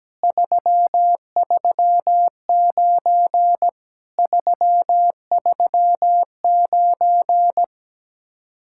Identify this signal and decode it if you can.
Morse code, heard as a keyed tone, about 17 wpm, 700 Hz. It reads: 339 339
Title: Speed-Racing Common QSO Phrases - 17wpm